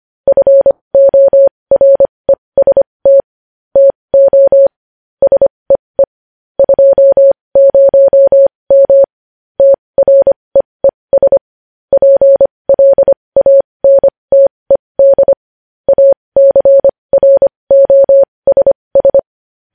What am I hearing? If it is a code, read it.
Morse code, heard as a keyed tone, about 25 wpm, spaced as for 20 wpm, 550 Hz. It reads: FOREST TO SEE 20M TREES PLANTED ACROSS
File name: podcast / News headlines-1x-25x20